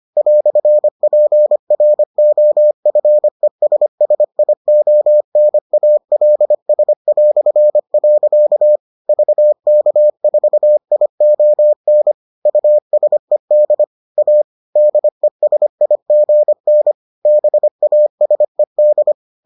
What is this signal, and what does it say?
Morse code, heard as a keyed tone, about 25 wpm, 600 Hz. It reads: "PROFESSIONALS". VK4ION USED A DESIGN BASED